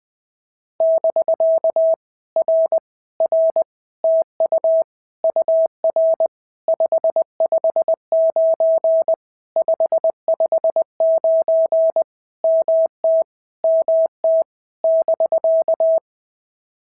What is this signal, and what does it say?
Morse code, heard as a keyed tone, about 20 wpm, 650 Hz. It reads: <BK> R R TU UR 559 559 MT MT <BK>
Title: POTA QSO Practice - 20wpm